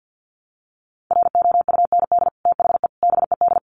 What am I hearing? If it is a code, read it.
Morse code, heard as a keyed tone, about 50 wpm, 700 Hz. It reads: ROUND THE BED
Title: Sentences from Top 1000 Words 50wpm